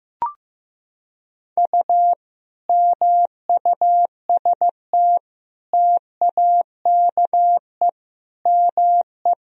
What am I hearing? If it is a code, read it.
Morse code, heard as a keyed tone, about 15 wpm, 700 Hz. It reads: U MUST TAKE ME
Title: Sentences from Top 200 Words and Abbreviations 15wpm